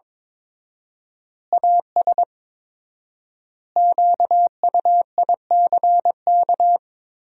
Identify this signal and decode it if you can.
Morse code, heard as a keyed tone, about 22 wpm, 700 Hz. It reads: AS QUICK
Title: 4x Word-Spacing - Sentences from Top 500 Words - 22wpm